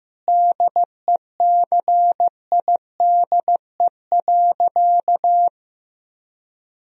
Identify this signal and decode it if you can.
Morse code, heard as a keyed tone, about 16 wpm, 700 Hz. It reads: DECIDE.
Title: Sentences from Top 1000 Words - Review Part 3 - 15wpm